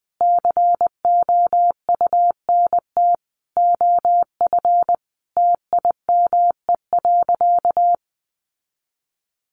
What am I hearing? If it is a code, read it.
Morse code, heard as a keyed tone, about 20 wpm, 700 Hz. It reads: COUNT OF TIME.
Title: Sentences from Top 1500 Words 20wpm